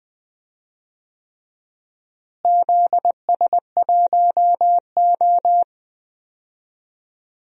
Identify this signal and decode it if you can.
Morse code, heard as a keyed tone, about 20 wpm, 700 Hz. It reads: ZS1O
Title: DXCC-100 Call Signs - 4 Characters - Repeated 3x - 20wpm